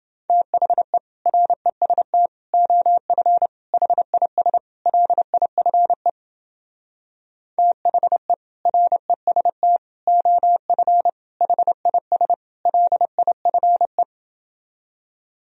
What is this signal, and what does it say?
Morse code, heard as a keyed tone, about 30 wpm, 700 Hz. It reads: THE REST OF HIS LIFE THE REST OF HIS LIFE